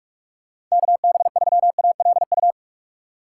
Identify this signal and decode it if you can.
Morse code, heard as a keyed tone, about 41 wpm, 700 Hz. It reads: KB3ALU
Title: USA Call Signs - 6 Characters - 2x Character Spacing - 45wpm